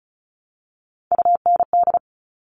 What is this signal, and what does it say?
Morse code, heard as a keyed tone, about 34 wpm, 700 Hz. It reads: UND
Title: All Cut Numbers - 3-Digit 35wpm